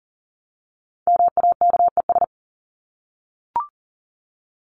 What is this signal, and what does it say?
Morse code, heard as a keyed tone, about 40 wpm, 700 Hz. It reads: MAKES
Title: ES:S Suffix - Top 100 Words with Base Words 40wpm